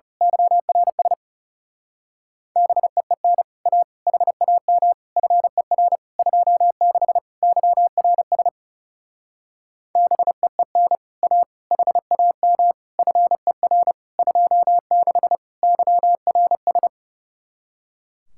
Morse code, heard as a keyed tone, about 34 wpm, 700 Hz. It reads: YRS BEEN A HAM FER 26 YRS BEEN A HAM FER 26 YRS
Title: Speed-Racing Common QSO Phrases Encore - 30wpm